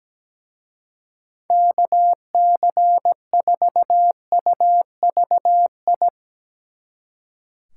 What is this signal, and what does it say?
Morse code, heard as a keyed tone, about 19 wpm, 700 Hz. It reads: KC4UVI
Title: Speed Racing - USA Call Signs 17wpm